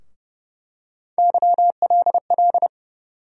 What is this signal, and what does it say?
Morse code, heard as a keyed tone, about 30 wpm, 700 Hz. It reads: YLL
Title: USA Call Sign Postfixes - 3 Letters 30wpm